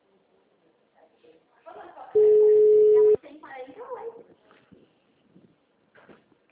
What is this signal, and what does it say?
Ringback tone (Europe, 425 Hz), with speech.